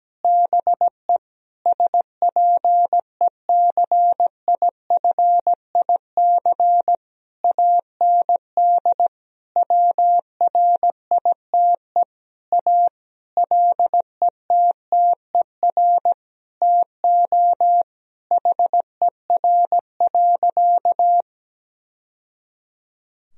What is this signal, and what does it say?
Morse code, heard as a keyed tone, about 17 wpm, 700 Hz. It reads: BE SPECIFIC AND WRITE A LETTER TO HER.
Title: Sentences from Top 1000 Words - Review Part 3 - 17wpm